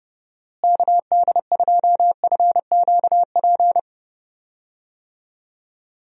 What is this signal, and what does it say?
Morse code, heard as a keyed tone, about 29 wpm, 700 Hz. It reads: KD2FQP
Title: USA Call Signs - 6 Characters - Repeated 3x - 30wpm